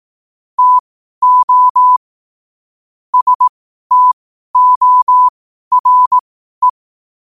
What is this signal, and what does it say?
Morse code, heard as a keyed tone, about 18 wpm, spaced as for 12 wpm, 1000 Hz. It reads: TO STORE
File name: qotdfiles / 12wpm